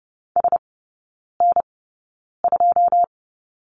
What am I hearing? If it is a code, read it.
Morse code, heard as a keyed tone, about 30 wpm, spaced as for 9 wpm, 700 Hz. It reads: SN2
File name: 081 Lesson 14e - Number 2 - Three Characters 30wpm